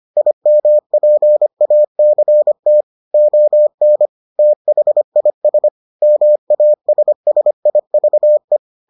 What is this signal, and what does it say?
Morse code, heard as a keyed tone, about 25 wpm, 600 Hz. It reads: IMPACT ON THIS MASSIVE